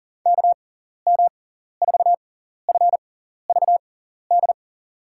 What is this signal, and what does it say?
Morse code, heard as a keyed tone, about 40 wpm, spaced as for 14 wpm, 700 Hz. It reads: KM4FVD
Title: USA Call Signs - 6x Character Spacing - 40wpm